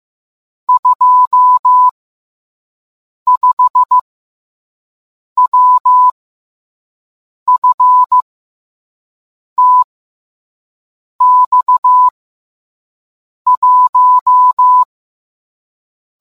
Morse code, heard as a keyed tone, about 15 wpm, 1000 Hz. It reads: 2 5 W F T X 1